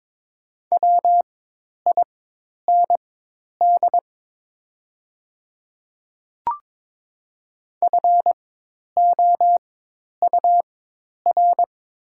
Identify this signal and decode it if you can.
Morse code, heard as a keyed tone, about 23 wpm, spaced as for 10 wpm, 700 Hz. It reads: WIND FOUR
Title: 4x Character Spacing - Top 500 Words - 22wpm